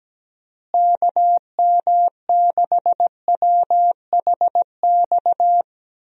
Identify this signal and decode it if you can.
Morse code, heard as a keyed tone, about 17 wpm, 700 Hz. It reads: KM6WHX
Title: USA Call Signs - 6 Characters - Repeated 3x - 17wpm